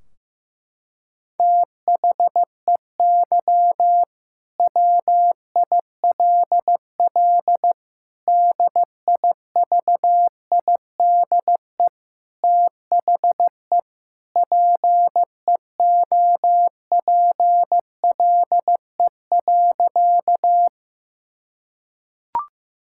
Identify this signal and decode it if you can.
Morse code, heard as a keyed tone, about 15 wpm, 700 Hz. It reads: THEY WILL DIVIDE THE PEOPLE.
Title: Sentences from Top 1600 Words 15wpm